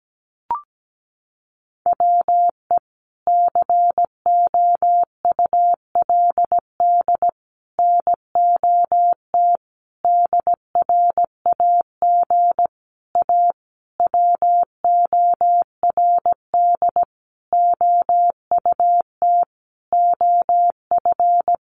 Morse code, heard as a keyed tone, about 17 wpm, 700 Hz. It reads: WE COULD NOT DRAG A WORD OUT OF
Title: Sentences from Top 1500 Words 17wpm